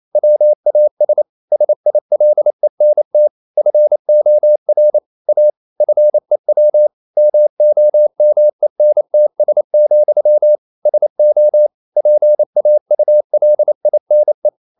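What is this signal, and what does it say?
Morse code, heard as a keyed tone, about 28 wpm, 600 Hz. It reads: WAS SILENT FOR A FEW MOMENTS, SO PAULINE
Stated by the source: Morse code practice at 28 wpm, text of Ken Follett's novel Never in English, from CW med Gnister.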